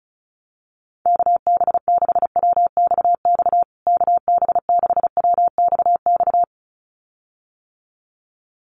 Morse code, heard as a keyed tone, about 34 wpm, 700 Hz. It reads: KB6WXX KB6WXX
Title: USA Call Signs Repeated 2x 35wpm